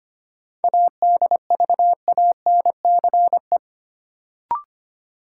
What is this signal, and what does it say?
Morse code, heard as a keyed tone, about 25 wpm, 700 Hz. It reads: ADVANCE
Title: Sentences from Top 1400 Words - New Word Drill - 25wpm